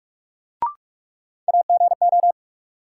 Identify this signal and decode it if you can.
Morse code, heard as a keyed tone, about 46 wpm, 700 Hz. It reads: AGO
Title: Sentences from Top 500 Words - New Word Drill - 45wpm